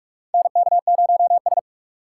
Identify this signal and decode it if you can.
Morse code, heard as a keyed tone, about 44 wpm, 700 Hz. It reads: NK0S
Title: USA Call Signs - 4 Characters - 2x Character Spacing - 45wpm